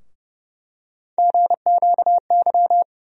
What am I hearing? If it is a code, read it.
Morse code, heard as a keyed tone, about 30 wpm, 700 Hz. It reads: GQY
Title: USA Call Sign Postfixes - 3 Letters 30wpm